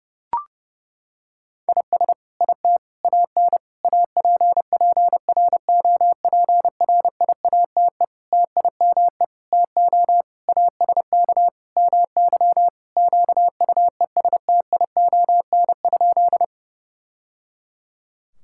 Morse code, heard as a keyed tone, about 32 wpm, 700 Hz. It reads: IS IT AN APPROPRIATE TIME TO ASK MY QUESTION?
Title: Sentences from Top 1000 Words - Review Part 5 - 30wpm